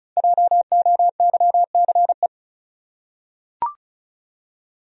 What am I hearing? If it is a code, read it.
Morse code, heard as a keyed tone, about 36 wpm, 700 Hz. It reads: JOYCE
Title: USA Common Names - Max 5 Letters - 35wpm